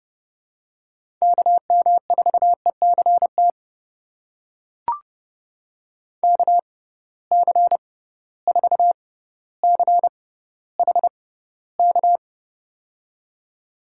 Morse code, heard as a keyed tone, about 31 wpm, spaced as for 15 wpm, 700 Hz. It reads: KM4ECT KC4CHK
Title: USA Call Signs - 6 Characters - 6x Character Spacing - 30wpm